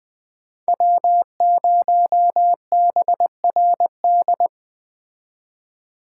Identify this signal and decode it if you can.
Morse code, heard as a keyed tone, about 21 wpm, 700 Hz. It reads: W0BRD